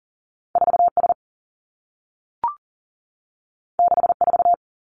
Morse code, heard as a keyed tone, about 40 wpm, 700 Hz. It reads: 4 S 6 4
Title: Two Challenging Letter-Number Combinations 40wpm